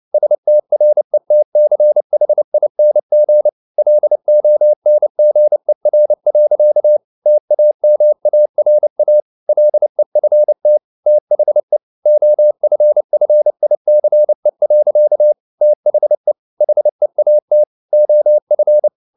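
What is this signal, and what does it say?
Morse code, heard as a keyed tone, about 30 wpm, 600 Hz. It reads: STRETCHING LONGER. TAMARA LEFT THE OFFICE. THE HEAT OF
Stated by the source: Morse code at 29 wpm, sending Ken Follett's novel Never in English, from CW med Gnister.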